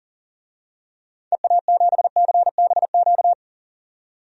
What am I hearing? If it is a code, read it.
Morse code, heard as a keyed tone, about 40 wpm, 700 Hz. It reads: EA7CBQ
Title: DXCC-100 Call Signs - 6 Characters - 40wpm